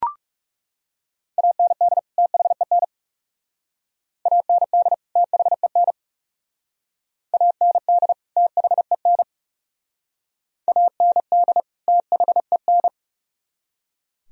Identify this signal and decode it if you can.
Morse code, heard as a keyed tone, about 38 wpm, 700 Hz. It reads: AND THEN AND THEN AND THEN AND THEN
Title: Speed-Racing - Sets of 2 Words 30wpm